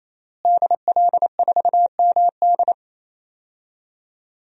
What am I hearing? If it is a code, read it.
Morse code, heard as a keyed tone, about 27 wpm, 700 Hz. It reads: DL4MD
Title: DXCC-100 Call Signs - 5 Characters - Repeated 3x - 28wpm